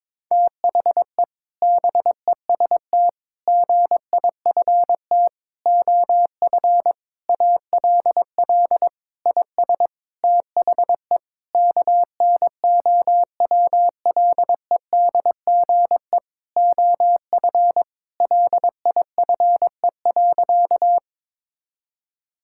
Morse code, heard as a keyed tone, about 22 wpm, 700 Hz. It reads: THE BEST GIFT OF ALL IS THE KNOWLEDGE OF LIFE.
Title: Sentences from Top 1000 Words - Review Part 6 - 22wpm